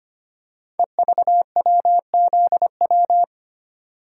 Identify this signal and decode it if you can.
Morse code, heard as a keyed tone, about 25 wpm, 690 Hz. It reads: EVWZW
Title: Code Groups - 5 Letter:Numbers - 25wpm